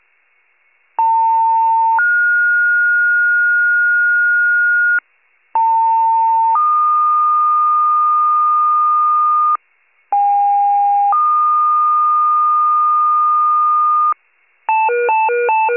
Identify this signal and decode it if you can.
Two-tone page, heard as a one-second tone, then a three-second tone, about 900 then 1430 Hz, about 900 then 1220 Hz, about 810 then 1220 Hz. It is heard from 1.0 to 14.1 s.
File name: wcalert